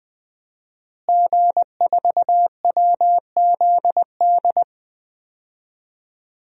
Morse code, heard as a keyed tone, about 20 wpm, 700 Hz. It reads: G4WZD
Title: DXCC-100 Call Signs - 5 Characters - Repeated 2x - 20wpm